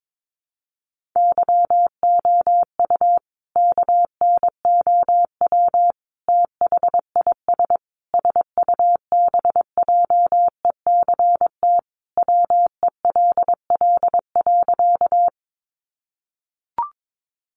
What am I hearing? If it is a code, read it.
Morse code, heard as a keyed tone, about 22 wpm, 700 Hz. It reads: YOU KNOW THIS SUBJECT WELL.
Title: Sentences from Top 800 Words 22wpm